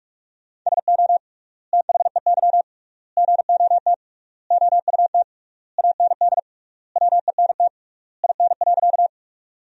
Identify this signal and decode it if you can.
Morse code, heard as a keyed tone, about 48 wpm, 700 Hz. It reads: SO THEY GOT OUT AND WENT IN.
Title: 3x Word-Spacing - Sentences from Top 300 Words - 45wpm